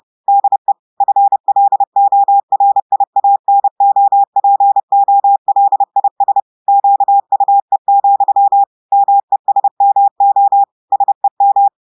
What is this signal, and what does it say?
Morse code, heard as a keyed tone, about 30 wpm, 800 Hz. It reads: DE FLORIANOPOLIS QUE, MESMO SEM
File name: audios / portugais / portugais texte en clair 1 vitesse 30 mots minutes